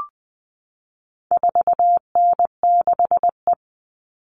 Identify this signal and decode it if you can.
Morse code, heard as a keyed tone, about 20 wpm, 700 Hz. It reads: 4N6E
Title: All Cut Numbers - 4-Digit 20wpm